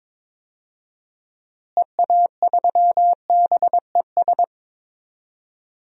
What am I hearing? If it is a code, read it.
Morse code, heard as a keyed tone, about 22 wpm, 700 Hz. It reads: EA3BES
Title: DXCC-100 Call Signs - 6 Characters - Repeated 3x - 22wpm